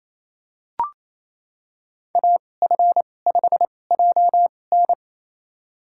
Morse code, heard as a keyed tone, about 28 wpm, spaced as for 20 wpm, 700 Hz. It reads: AF5JN
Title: USA Call Signs - 5 Characters - 2x Character Spacing - 28wpm